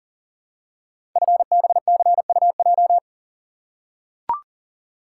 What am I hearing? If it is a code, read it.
Morse code, heard as a keyed tone, about 39 wpm, 690 Hz. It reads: FBCUJ
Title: Code Groups - 5 Letters - 40wpm